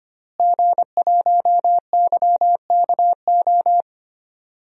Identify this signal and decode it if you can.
Morse code, heard as a keyed tone, about 25 wpm, 700 Hz. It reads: G1YKO
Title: DXCC-100 Call Signs - 5 Characters - Repeated 3x - 25wpm